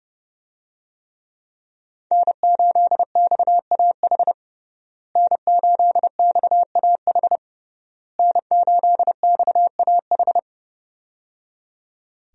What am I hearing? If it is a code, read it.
Morse code, heard as a keyed tone, about 30 wpm, 700 Hz. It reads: N8XAH N8XAH N8XAH
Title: USA Call Signs - 5 Characters - Repeated 3x - 30wpm